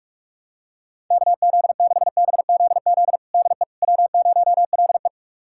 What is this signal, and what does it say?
Morse code, heard as a keyed tone, about 46 wpm, 700 Hz. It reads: KZ6BZZ DE W0LE
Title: USA Callsign DE Callsign 45wpm